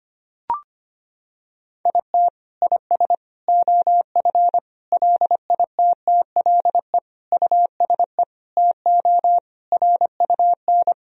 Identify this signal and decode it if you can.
Morse code, heard as a keyed tone, about 25 wpm, 700 Hz. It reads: IT IS OF LITTLE USE TO RUN
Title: Sentences from Top 1000 Words - Review Part 1 - 25wpm